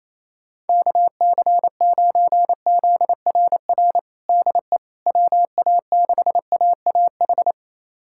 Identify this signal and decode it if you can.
Morse code, heard as a keyed tone, about 28 wpm, 700 Hz. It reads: KC9ZRR DE WA6AAH
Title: USA Callsign DE Callsign 28wpm